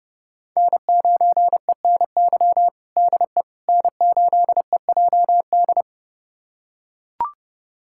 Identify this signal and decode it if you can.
Morse code, heard as a keyed tone, about 30 wpm, 700 Hz. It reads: N9ENY DE N8EJD
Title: USA Callsign DE Callsign - 5 Characters - 30wpm